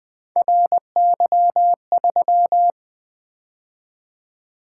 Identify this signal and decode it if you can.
Morse code, heard as a keyed tone, about 20 wpm, 700 Hz. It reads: RY3
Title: Three Letter-Number Combinations - 20wpm